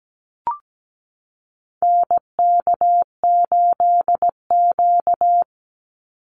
Morse code, heard as a keyed tone, about 17 wpm, 700 Hz. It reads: NK8Q
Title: NA SOTA Activator Callsigns 17wpm